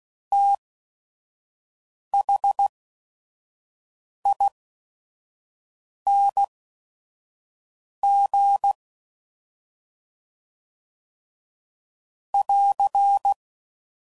Morse code, heard as a keyed tone, about 16 wpm, spaced as for 5 wpm, 800 Hz. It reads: THING <AR>
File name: Suffixes / Special ING's / SlowToFaster ING's